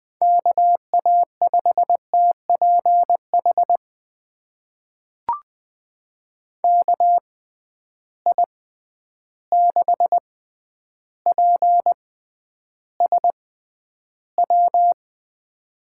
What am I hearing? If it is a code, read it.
Morse code, heard as a keyed tone, about 20 wpm, spaced as for 10 wpm, 700 Hz. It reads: KA5TPH KI6PSW
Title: USA Call Signs - 6 Characters - 6x Character Spacing - 20wpm